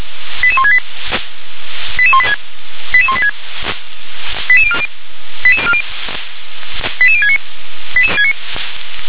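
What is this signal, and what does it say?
Five-tone selective call (ZVEI), heard as quick runs of five tones, tones of about 70 ms. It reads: 88168 88186 88186 88040 88069
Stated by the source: ZVEI five-tone selective call received by SDR.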